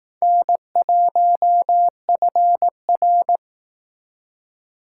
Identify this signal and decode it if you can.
Morse code, heard as a keyed tone, about 18 wpm, 700 Hz. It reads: N1FR
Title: Speed Racing - USA 1x2 Call Signs 15wpm